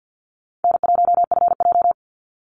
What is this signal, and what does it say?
Morse code, heard as a keyed tone, about 50 wpm, 700 Hz. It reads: N1FJ
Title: NA SOTA Activator Callsigns 50wpm